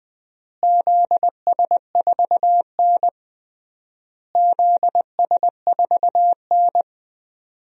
Morse code, heard as a keyed tone, about 20 wpm, 700 Hz. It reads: ZS4N ZS4N
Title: DXCC-100 Call Signs - 4 Characters - Repeated 3x - 20wpm